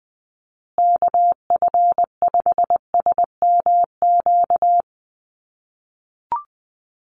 Morse code, heard as a keyed tone, about 20 wpm, 700 Hz. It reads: KF5SMQ
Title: USA Call Signs Repeated 3x 20wpm